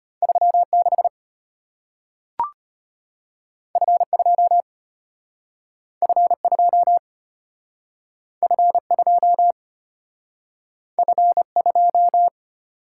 Morse code, heard as a keyed tone, about 34 wpm, 700 Hz. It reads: 36 F2 F2 F2 F2